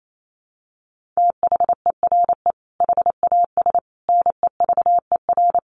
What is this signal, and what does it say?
Morse code, heard as a keyed tone, about 28 wpm, 700 Hz. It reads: THERE HAS NEVER
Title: Sentences from Top 1500 Words 28wpm